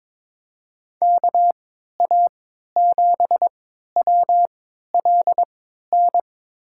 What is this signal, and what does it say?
Morse code, heard as a keyed tone, about 22 wpm, spaced as for 13 wpm, 700 Hz. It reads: KA7WLN
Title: USA Call Signs - 3x Character Spacing - 22wpm